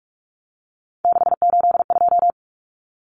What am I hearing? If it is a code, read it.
Morse code, heard as a keyed tone, about 44 wpm, 700 Hz. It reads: 682
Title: Common Cut Numbers - 3-Digit 45wpm